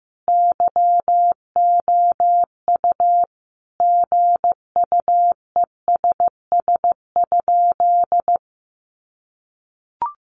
Morse code, heard as a keyed tone, about 15 wpm, 700 Hz. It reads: YOU GUESS?
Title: Sentences from Top 1500 Words 15wpm